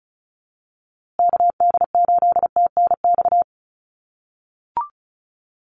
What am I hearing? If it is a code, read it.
Morse code, heard as a keyed tone, about 34 wpm, 700 Hz. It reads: KD8TNX
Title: USA Call Signs Repeated 3x 35wpm